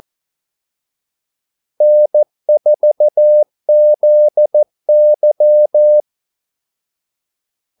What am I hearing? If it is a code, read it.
Morse code, heard as a keyed tone, about 14 wpm, 600 Hz. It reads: N4ZY
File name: Callsigns-14wpm